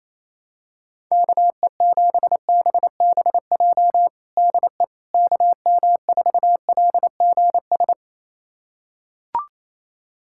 Morse code, heard as a keyed tone, about 28 wpm, 700 Hz. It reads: KE7BBJ DE KM4LGS
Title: USA Callsign DE Callsign 28wpm